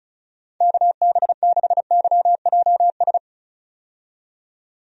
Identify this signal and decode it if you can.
Morse code, heard as a keyed tone, about 34 wpm, 700 Hz. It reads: KB6YJS
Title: USA Call Signs - 6 Characters - 35wpm